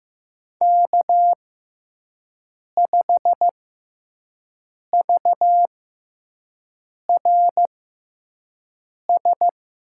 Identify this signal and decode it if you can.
Morse code, heard as a keyed tone, about 15 wpm, spaced as for 5 wpm, 700 Hz. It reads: K5VRS
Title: USA Call Signs - 6x Character Spacing - 15wpm